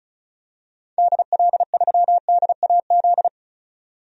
Morse code, heard as a keyed tone, about 36 wpm, 700 Hz. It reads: DL3DAZ
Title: DXCC-100 Call Signs - 6 Characters - 35wpm